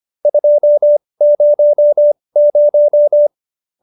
Morse code, heard as a keyed tone, about 25 wpm, spaced as for 20 wpm, 570 Hz. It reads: 200